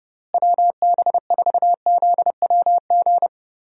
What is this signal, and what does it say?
Morse code, heard as a keyed tone, about 30 wpm, 700 Hz. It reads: WB4ZWG
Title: USA Call Signs - 6 Characters - Repeated 3x - 30wpm